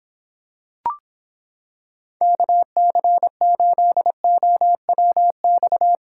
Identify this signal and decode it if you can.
Morse code, heard as a keyed tone, about 26 wpm, 700 Hz. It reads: KC8OWX